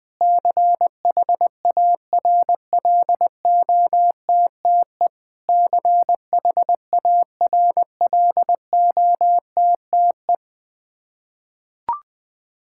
Morse code, heard as a keyed tone, about 21 wpm, 700 Hz. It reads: CHARLOTTE CHARLOTTE
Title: Practice QSO - Exchange 1 - Beginning - 20wpm